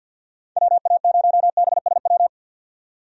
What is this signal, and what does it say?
Morse code, heard as a keyed tone, about 51 wpm, 700 Hz. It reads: WA0BSW
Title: USA Call Signs - 6 Characters - 50wpm